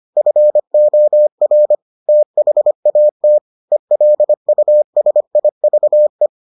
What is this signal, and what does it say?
Morse code, heard as a keyed tone, about 25 wpm, 600 Hz. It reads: FOR THAT ELUSIVE